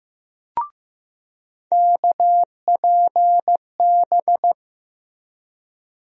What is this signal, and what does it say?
Morse code, heard as a keyed tone, about 15 wpm, 700 Hz. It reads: KPB
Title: USA Call Sign Postfixes - 3 Letters 15wpm